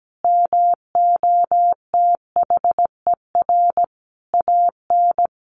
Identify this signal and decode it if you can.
Morse code, heard as a keyed tone, about 17 wpm, 700 Hz. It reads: MOTHER AN
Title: Sentences from Top 1500 Words 17wpm